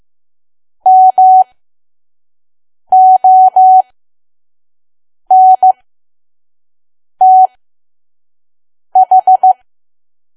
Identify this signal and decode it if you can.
Morse code, heard as a keyed tone, about 15 wpm, spaced as for 5 wpm, 750 Hz. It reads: MONTH
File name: Archive / 5 WPM files / PM